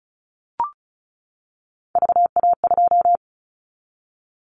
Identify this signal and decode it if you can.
Morse code, heard as a keyed tone, about 34 wpm, 700 Hz. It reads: VA2
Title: Callsign Prefixes for Top 50 DXCC - 35wpm